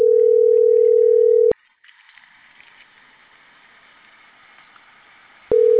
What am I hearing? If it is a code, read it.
Ringback tone (North America), 440 + 480 Hz.